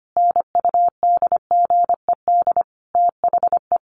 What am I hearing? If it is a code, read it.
Morse code, heard as a keyed tone, about 25 wpm, 700 Hz. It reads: NUDGED THE
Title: A Sound Of Thunder 25wpm